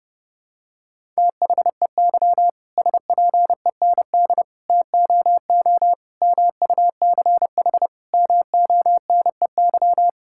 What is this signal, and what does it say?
Morse code, heard as a keyed tone, about 30 wpm, 700 Hz. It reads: THEY SPEND TOO MUCH MONEY
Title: Sentences from Top 1000 Words - Review Part 5 - 30wpm